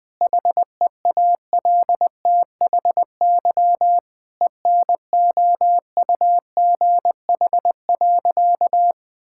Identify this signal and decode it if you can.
Morse code, heard as a keyed tone, about 20 wpm, 700 Hz. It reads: HEALTHY ENOUGH.
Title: Sentences from Top 1000 Words - Review Part 5 - 20wpm